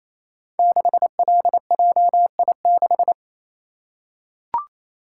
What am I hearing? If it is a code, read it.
Morse code, heard as a keyed tone, about 28 wpm, 700 Hz. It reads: 6LJI6
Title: Code Groups - 5 Letter:Numbers - 28wpm